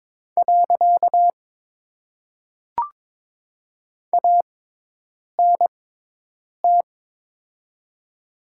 Morse code, heard as a keyed tone, about 22 wpm, 700 Hz. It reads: . A N T
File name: Top 90 QSO Elements - 6x Character Spacing - 22wpm